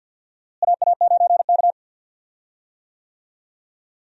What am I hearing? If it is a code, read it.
Morse code, heard as a keyed tone, about 51 wpm, 700 Hz. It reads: AA9K
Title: USA Call Signs - 4 Characters - Repeated 2x - 50wpm